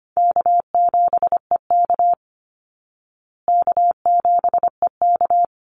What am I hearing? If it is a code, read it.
Morse code, heard as a keyed tone, about 25 wpm, 700 Hz. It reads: K7EK K7EK
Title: Easy Call Signs Repeated 2x - 25wpm